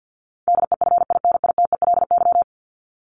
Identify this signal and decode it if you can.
Morse code, heard as a keyed tone, about 50 wpm, 700 Hz. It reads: DEFINITELY
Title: Top 1751-2000 Words 50wpm